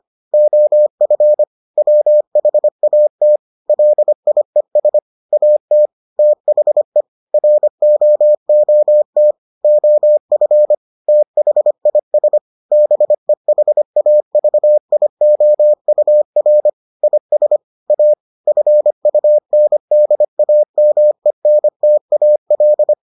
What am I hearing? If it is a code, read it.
Morse code, heard as a keyed tone, about 25 wpm, 600 Hz. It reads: OF WHAT LIES AT THE ROOT OF THIS BEHAVIOUR IS A FUNDAMENTAL